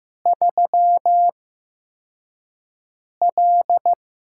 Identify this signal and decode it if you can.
Morse code, heard as a keyed tone, about 15 wpm, spaced as for 4 wpm, 700 Hz. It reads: 3L
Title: USA Call Signs - 8x Character Spacing - 15wpm